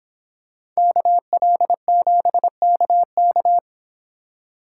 Morse code, heard as a keyed tone, about 23 wpm, 700 Hz. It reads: KL7KK